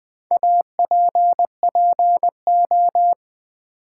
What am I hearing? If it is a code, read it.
Morse code, heard as a keyed tone, about 20 wpm, 700 Hz. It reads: APPO
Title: Build A Word - 4 Letters At A Time - 11-Letter Words - 20wpm